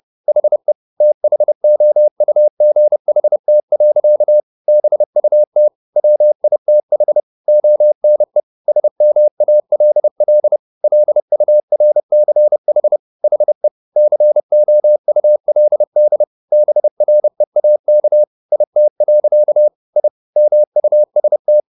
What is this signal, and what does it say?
Morse code, heard as a keyed tone, about 30 wpm, 600 Hz. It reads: HE THOUGHT. BUT WITH ONE SMALL LURCH HE COULD BREAK IT. I MUST